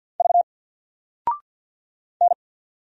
Morse code, heard as a keyed tone, about 50 wpm, 700 Hz. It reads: V N
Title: Single Letter-Number 50wpm